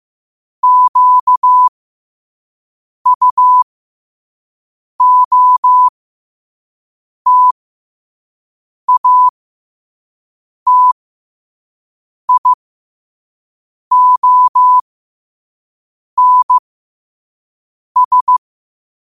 Morse code, heard as a keyed tone, about 15 wpm, spaced as for 5 wpm, 1000 Hz. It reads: QUOTATIONS
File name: qotdfiles / 5wpm